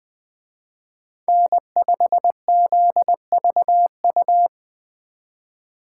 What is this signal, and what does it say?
Morse code, heard as a keyed tone, about 24 wpm, 700 Hz. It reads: N5ZVU
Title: Speed Racing - USA 1x3 Call Signs 20wpm